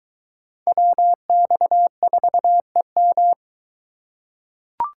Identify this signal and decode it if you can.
Morse code, heard as a keyed tone, about 21 wpm, 700 Hz. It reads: WX4EM